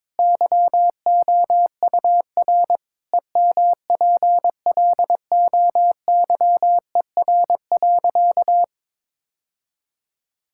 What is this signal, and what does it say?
Morse code, heard as a keyed tone, about 22 wpm, 700 Hz. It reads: YOUR EMPLOYER.
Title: Sentences from Top 1600 Words 22wpm